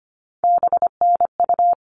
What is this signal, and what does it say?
Morse code, heard as a keyed tone, about 25 wpm, 700 Hz. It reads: BNU
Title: All Cut Numbers - 3-Digit 25wpm